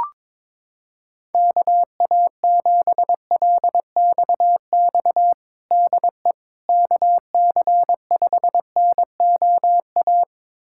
Morse code, heard as a keyed tone, about 22 wpm, 700 Hz. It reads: KA7LXX DE KC5NOA
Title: USA Callsign DE Callsign 22wpm